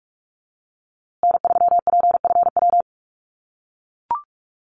Morse code, heard as a keyed tone, about 44 wpm, 700 Hz. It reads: N3PFW
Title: USA Call Signs Repeated 3x 45wpm